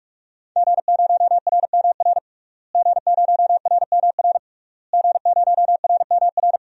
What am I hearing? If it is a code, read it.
Morse code, heard as a keyed tone, about 43 wpm, 700 Hz. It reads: G0RMR G0RMR G0RMR
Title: DXCC-100 Call Signs - 5 Characters - Repeated 3x - 45wpm